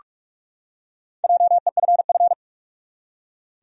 Morse code, heard as a keyed tone, about 44 wpm, 700 Hz. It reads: JEFF
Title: USA Common Names - Max 5 Letters - 45wpm